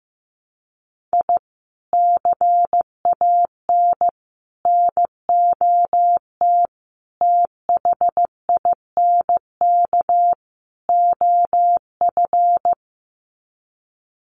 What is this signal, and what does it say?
Morse code, heard as a keyed tone, about 15 wpm, 700 Hz. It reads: I CAN NOT THINK OF
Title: Sets of 5 Words 15wpm